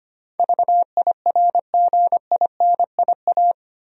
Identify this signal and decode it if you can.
Morse code, heard as a keyed tone, about 25 wpm, 700 Hz. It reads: VIRGINIA
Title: USA Common Names - 25wpm